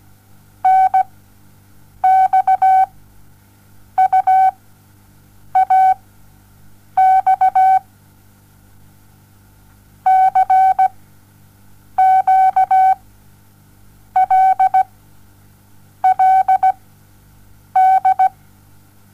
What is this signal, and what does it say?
Morse code, heard as a keyed tone, about 17 wpm, spaced as for 6 wpm, 760 Hz. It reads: NXUAX CQLLD
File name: Onlinelehrgang / morsen / DJ4UF Morsen07